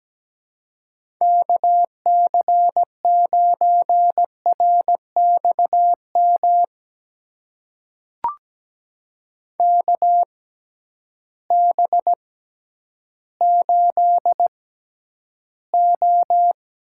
Morse code, heard as a keyed tone, about 17 wpm, spaced as for 10 wpm, 700 Hz. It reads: KC9RXM KB8O
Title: USA Call Signs - 6 Characters - 6x Character Spacing - 17wpm